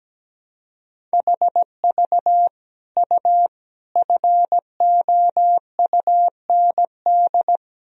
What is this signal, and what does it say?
Morse code, heard as a keyed tone, about 17 wpm, 700 Hz. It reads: HV U FOUND
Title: Sentences from Top 300 Words and Abbreviations 17wpm